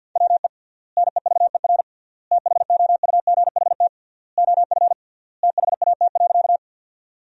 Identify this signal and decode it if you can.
Morse code, heard as a keyed tone, about 50 wpm, 700 Hz. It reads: WE NEVER THOUGHT OF THAT.
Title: 3x Word-Spacing - Sentences from Top 300 Words - 50wpm